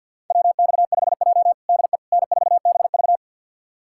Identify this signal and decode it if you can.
Morse code, heard as a keyed tone, about 52 wpm, 700 Hz. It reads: WX5J DE N4BV
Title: USA Callsign DE Callsign - 4 Characters - 50wpm